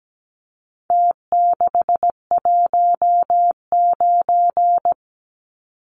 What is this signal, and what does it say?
Morse code, heard as a keyed tone, about 17 wpm, 700 Hz. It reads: T619
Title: Numbers - Contest Style - 4-Digit 17wpm